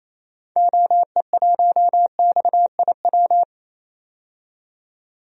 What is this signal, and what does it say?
Morse code, heard as a keyed tone, about 28 wpm, 700 Hz. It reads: OE1XIW
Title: DXCC-100 Call Signs - 6 Characters - 28wpm